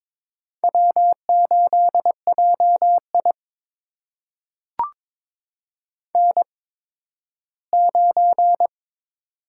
Morse code, heard as a keyed tone, about 22 wpm, spaced as for 13 wpm, 700 Hz. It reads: W8JI N9
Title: USA Call Signs - 4 Characters - 8x Character Spacing - 22wpm